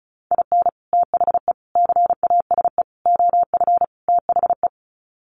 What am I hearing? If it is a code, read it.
Morse code, heard as a keyed tone, about 34 wpm, 700 Hz. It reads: IN THE CASE OF THE
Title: Sets of 5 Words 35wpm